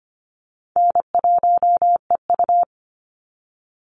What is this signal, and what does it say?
Morse code, heard as a keyed tone, about 25 wpm, 700 Hz. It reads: N1EU
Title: NA SOTA Activator Callsigns 25wpm